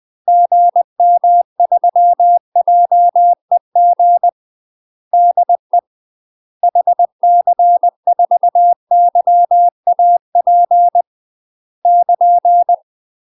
Morse code, heard as a keyed tone, about 20 wpm, 700 Hz. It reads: GM3JEG DE HC4YAP <KN>
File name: UK MASTER QSOs TRAINING/20WPM